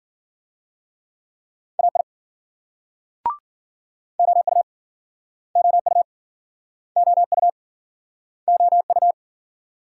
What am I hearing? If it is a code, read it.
Morse code, heard as a keyed tone, about 51 wpm, 700 Hz. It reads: AI OU OU OU OU